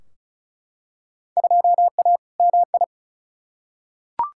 Morse code, heard as a keyed tone, about 34 wpm, 700 Hz. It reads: 2A MI
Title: Field Day Exchanges - 35wpm